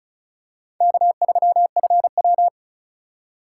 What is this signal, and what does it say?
Morse code, heard as a keyed tone, about 36 wpm, 700 Hz. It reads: K3FW
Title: USA Call Signs - 4 Characters - 2x Character Spacing - 35wpm